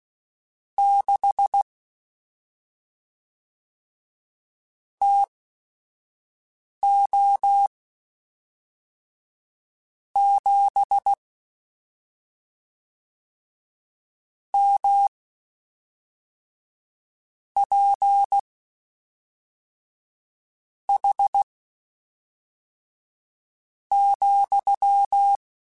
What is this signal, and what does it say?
Morse code, heard as a keyed tone, about 16 wpm, spaced as for 4 wpm, 800 Hz. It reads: 6 TO 7 MPH,
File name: Spaced Characters / Spaced Characters-mp3s / Weather4 24 sp @5